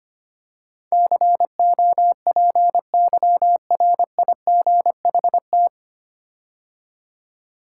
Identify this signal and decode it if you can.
Morse code, heard as a keyed tone, about 25 wpm, 700 Hz. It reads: COPYRIGHT
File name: practice / 25wpm / 9 Letter Words - First 3 Letters Given - 25wpm